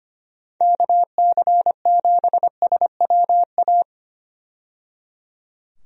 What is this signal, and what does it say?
Morse code, heard as a keyed tone, about 25 wpm, 700 Hz. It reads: KC7SWA
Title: USA Call Signs - 6 Characters - 25wpm